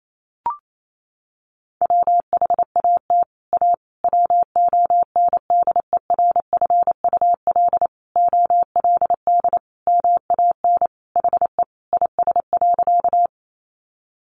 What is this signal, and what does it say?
Morse code, heard as a keyed tone, about 28 wpm, 700 Hz. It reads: WHAT A WONDERFUL OLD MAN HE IS.
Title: Sentences from Top 1100 Words 28wpm